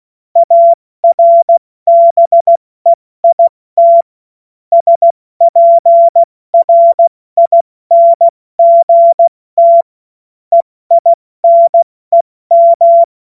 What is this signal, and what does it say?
Morse code, heard as a keyed tone, about 16 wpm, 680 Hz. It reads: ARBEIT SPRINGT EINEM
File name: Morsekurs / Witze / Witz 05. April 2022 70 F80